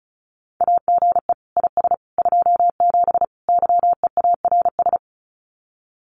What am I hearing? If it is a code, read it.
Morse code, heard as a keyed tone, about 34 wpm, 700 Hz. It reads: AGE IS 27 YEARS
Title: Common QSO Phrases Encore 35wpm